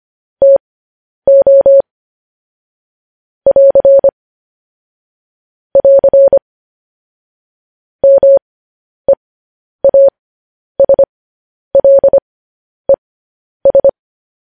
Morse code, heard as a keyed tone, about 25 wpm, spaced as for 10 wpm, 550 Hz. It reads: TO <AR> <AR> MEASLES
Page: news headlines in morse code at 10 wpm